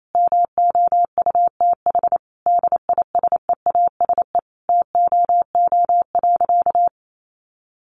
Morse code, heard as a keyed tone, about 28 wpm, 700 Hz. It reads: MOUTH DISEASE TOO.
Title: Sentences from Top 1500 Words 28wpm